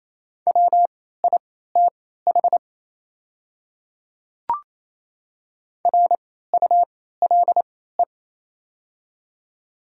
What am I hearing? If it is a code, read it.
Morse code, heard as a keyed tone, about 28 wpm, spaced as for 16 wpm, 700 Hz. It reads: WITH RULE
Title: 3x Character Spacing - Top 500 Words - 28wpm